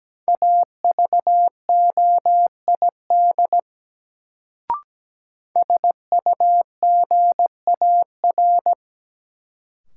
Morse code, heard as a keyed tone, about 17 wpm, 700 Hz. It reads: AVOID SUGAR
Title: Sentences from Top 1200 Words - New Word Drill Alt - 17wpm